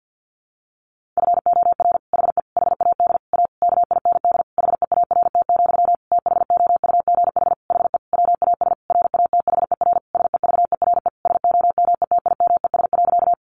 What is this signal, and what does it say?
Morse code, heard as a keyed tone, about 50 wpm, 700 Hz. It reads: FOR HE HAD A KIND HEART, THOUGH HE WAS RATHER SEVERE SOMETIMES.
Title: Sentences from Top 1400 Words 50wpm